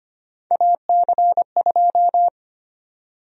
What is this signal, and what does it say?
Morse code, heard as a keyed tone, about 25 wpm, 700 Hz. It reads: AC2
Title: USA Call Sign Prefixes - Repeated 2x - 25wpm